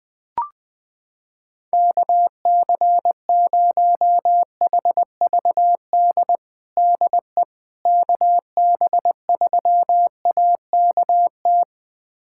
Morse code, heard as a keyed tone, about 20 wpm, 700 Hz. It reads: KC0HVD DE KB3AKT
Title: USA Callsign DE Callsign 20wpm